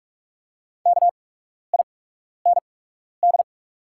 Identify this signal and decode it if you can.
Morse code, heard as a keyed tone, about 48 wpm, spaced as for 12 wpm, 690 Hz. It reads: KIND
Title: 8x Character Spacing - Top 300 Words - 45wpm